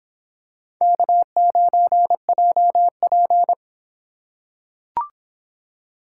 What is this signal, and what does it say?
Morse code, heard as a keyed tone, about 25 wpm, 700 Hz. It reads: K9JP